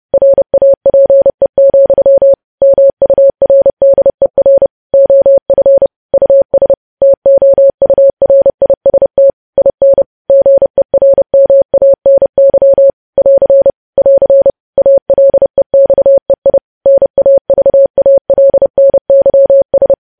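Morse code, heard as a keyed tone, about 30 wpm, 550 Hz. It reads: RAPE, MURDER OF US TOURIST IN GERMANY <AR> <AR> ALEXEI NAVALNYS
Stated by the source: News headlines sent as Morse practice at 30.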